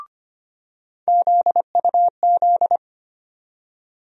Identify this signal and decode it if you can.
Morse code, heard as a keyed tone, about 25 wpm, 700 Hz. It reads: ZUZ
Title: USA Call Sign Postfixes - 3 Letters 25wpm